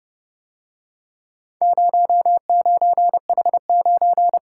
Morse code, heard as a keyed tone, about 30 wpm, 700 Hz. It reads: 09H9